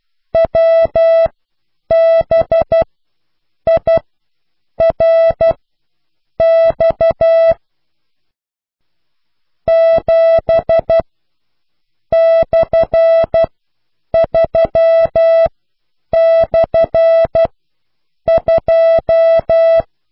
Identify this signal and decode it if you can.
Morse code, heard as a keyed tone, about 12 wpm, spaced as for 7 wpm, 660 Hz. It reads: WBIRX 7/3/2